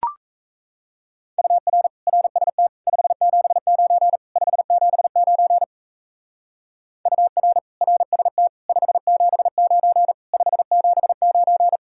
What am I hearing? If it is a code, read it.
Morse code, heard as a keyed tone, about 35 wpm, 700 Hz. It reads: UR RST 579 579 UR RST 579 579
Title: Speed-Racing Common QSO Phrases - 28wpm